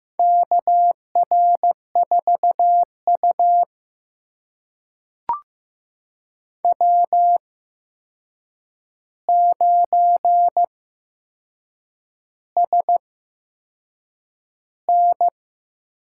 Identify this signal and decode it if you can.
Morse code, heard as a keyed tone, about 15 wpm, spaced as for 6 wpm, 700 Hz. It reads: KR4U W9SN
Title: USA Call Signs - 4 Characters - 8x Character Spacing - 15wpm